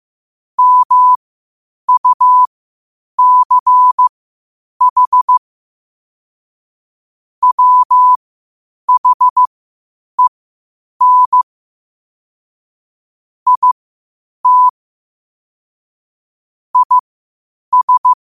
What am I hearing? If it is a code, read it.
Morse code, heard as a keyed tone, about 15 wpm, spaced as for 8 wpm, 1000 Hz. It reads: MUCH WHEN IT IS
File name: qotdfiles / 8wpm